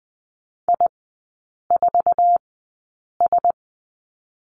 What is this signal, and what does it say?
Morse code, heard as a keyed tone, about 20 wpm, spaced as for 8 wpm, 700 Hz. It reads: I4S
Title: 047 Lesson 9e - Number 4 - Three Characters 20wpm